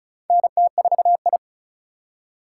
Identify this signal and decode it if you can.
Morse code, heard as a keyed tone, about 34 wpm, 700 Hz. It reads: NT4I
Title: USA Call Signs - 4 Characters - 4x Character Spacing - 35wpm